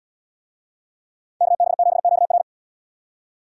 Morse code, heard as a keyed tone, about 77 wpm, 700 Hz. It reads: KB7QR